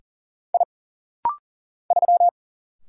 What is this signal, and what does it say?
Morse code, heard as a keyed tone, about 40 wpm, 700 Hz. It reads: I 3
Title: Single Letter-Number Alt - Rapid-Fire 40wpm